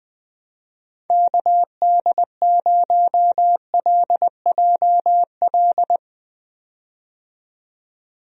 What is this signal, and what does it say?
Morse code, heard as a keyed tone, about 20 wpm, 700 Hz. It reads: KD0LJL
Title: USA Call Signs - 6 Characters - 20wpm